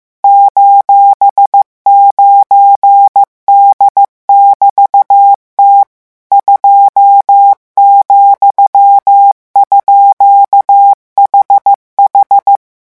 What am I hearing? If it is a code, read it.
Morse code, heard as a keyed tone, about 15 wpm, 800 Hz. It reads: :9D<BT>T 2,_HH